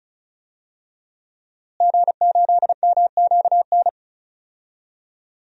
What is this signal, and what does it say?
Morse code, heard as a keyed tone, about 36 wpm, 700 Hz. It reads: G8MQN
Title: DXCC-100 Call Signs - 5 Characters - Repeated 3x - 35wpm